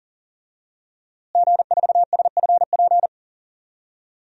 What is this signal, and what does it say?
Morse code, heard as a keyed tone, about 40 wpm, 700 Hz. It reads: G4SFP
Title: DXCC-100 Call Signs - 5 Characters - Repeated 2x - 40wpm